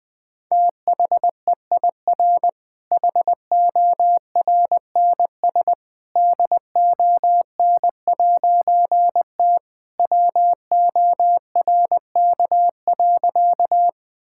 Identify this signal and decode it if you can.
Morse code, heard as a keyed tone, about 21 wpm, 700 Hz. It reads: THEIR HORNS DON'T WORK.
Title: All-Age Appropriate Jokes - 20wpm